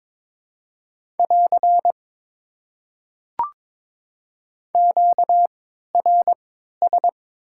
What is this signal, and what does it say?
Morse code, heard as a keyed tone, about 22 wpm, spaced as for 13 wpm, 700 Hz. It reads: <AR> QRS
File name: Top 90 QSO Elements - 3x Character Spacing - 22wpm